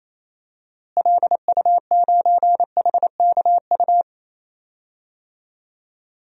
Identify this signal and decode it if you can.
Morse code, heard as a keyed tone, about 28 wpm, 700 Hz. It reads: LU9HKU
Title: DXCC-100 Call Signs - 6 Characters - Repeated 3x - 28wpm